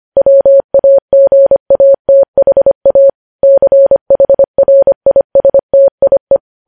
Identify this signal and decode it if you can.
Morse code, heard as a keyed tone, about 25 wpm, 550 Hz. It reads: WAGATHA CHRISTIE
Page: news headlines in morse code at 25 wpm